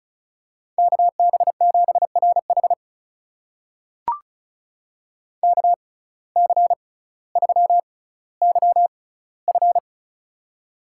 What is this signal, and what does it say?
Morse code, heard as a keyed tone, about 34 wpm, spaced as for 18 wpm, 700 Hz. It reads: KB7RH KC3YF
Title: USA Call Signs - 5 Characters - 6x Character Spacing - 35wpm